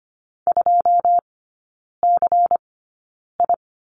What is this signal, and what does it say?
Morse code, heard as a keyed tone, about 25 wpm, spaced as for 9 wpm, 700 Hz. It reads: 2CI
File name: course / 25:10wpm / 093 Lesson 16e - Letter c - Three Characters 25wpm